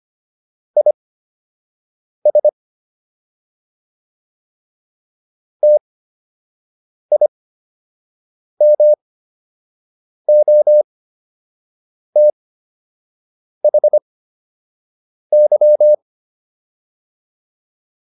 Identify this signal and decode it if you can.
Morse code, heard as a keyed tone, about 25 wpm, spaced as for 6 wpm, 600 Hz. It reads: IS TIMOTHY